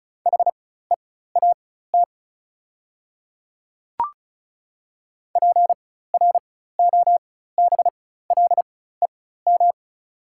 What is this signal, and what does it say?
Morse code, heard as a keyed tone, about 36 wpm, spaced as for 16 wpm, 700 Hz. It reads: HEAT PROBLEM
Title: 4x Character Spacing - Top 500 Words - 35wpm